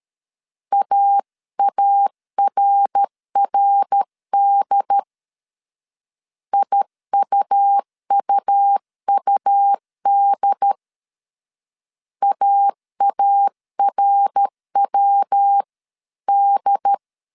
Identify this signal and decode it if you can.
Morse code, heard as a keyed tone, about 13 wpm, 800 Hz. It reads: AARRD IUUUD AARWD
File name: CW / DET / 1 3 1500C S A 12wpm 2905